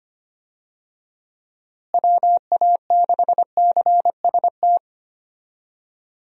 Morse code, heard as a keyed tone, about 25 wpm, 700 Hz. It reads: WA6CST
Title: USA Call Signs - 6 Characters - Repeated 3x - 25wpm